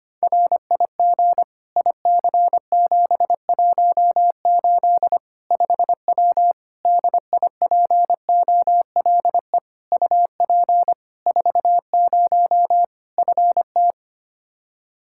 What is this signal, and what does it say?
Morse code, heard as a keyed tone, about 25 wpm, 700 Hz. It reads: RIG IC718 5W DIPOLE UP 40 FT
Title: K1USN SST QSO Practice 25wpm